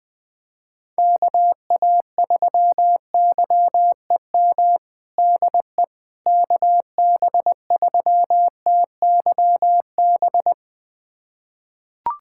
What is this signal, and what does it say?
Morse code, heard as a keyed tone, about 20 wpm, 700 Hz. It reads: KA3YEM DE KB3TYB
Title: USA Callsign DE Callsign 20wpm